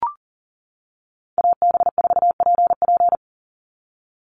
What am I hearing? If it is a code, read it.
Morse code, heard as a keyed tone, about 40 wpm, 700 Hz. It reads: AB4PP
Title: NA SOTA Chaser Callsigns 40wpm